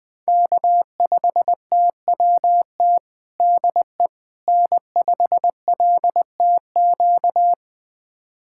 Morse code, heard as a keyed tone, about 20 wpm, 700 Hz. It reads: K5TWT DE N5LTQ
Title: USA Callsign DE Callsign - 5 Characters - 20wpm